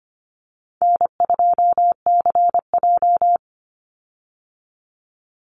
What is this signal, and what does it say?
Morse code, heard as a keyed tone, about 25 wpm, 700 Hz. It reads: N2CJ
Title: Call Signs - 25wpm